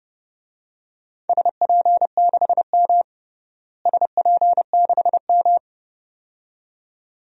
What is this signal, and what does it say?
Morse code, heard as a keyed tone, about 30 wpm, 700 Hz. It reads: SP6M SP6M
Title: DXCC-100 Call Signs - 4 Characters - Repeated 2x - 30wpm